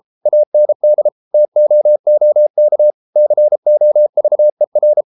Morse code, heard as a keyed tone, about 33 wpm, 600 Hz. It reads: AND TOOK COVER